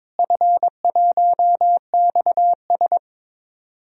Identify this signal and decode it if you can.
Morse code, heard as a keyed tone, about 22 wpm, 700 Hz. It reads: F1XS
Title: DXCC-100 Call Signs - 4 Characters - Repeated 2x - 22wpm